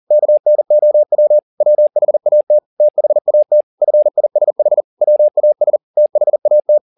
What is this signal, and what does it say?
Morse code, heard as a keyed tone, about 40 wpm, 600 Hz. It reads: KNOW WHAT THAT FISH WAS THAT